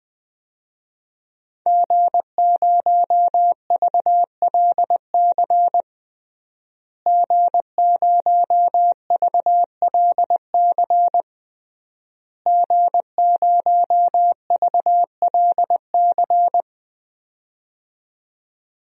Morse code, heard as a keyed tone, about 20 wpm, 700 Hz. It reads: G0VLC G0VLC G0VLC
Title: DXCC-100 Call Signs - 5 Characters - Repeated 3x - 20wpm